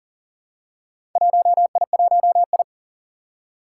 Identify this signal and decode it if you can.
Morse code, heard as a keyed tone, about 38 wpm, 700 Hz. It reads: 1I1I
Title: Four Challenging Letter-Number Combinations 40wpm